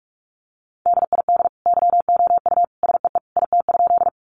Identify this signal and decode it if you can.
Morse code, heard as a keyed tone, about 44 wpm, 700 Hz. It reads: DID YOU SEE IT?
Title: Sentences from Top 100 Words 45wpm